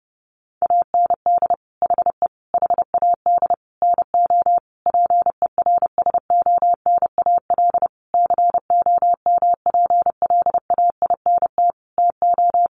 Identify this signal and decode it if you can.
Morse code, heard as a keyed tone, about 30 wpm, 700 Hz. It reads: AND HE HAD NO PERSONAL COMPLAINT TO
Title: Sentences from Top 1400 Words 30wpm